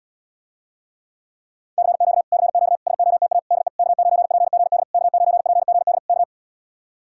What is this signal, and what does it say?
Morse code, heard as a keyed tone, about 77 wpm, 700 Hz. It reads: CQ CQ SOTA DE K9PGD K9PGD K